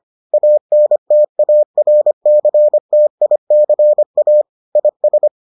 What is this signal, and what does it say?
Morse code, heard as a keyed tone, about 25 wpm, 600 Hz. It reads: ANTARCTICA IS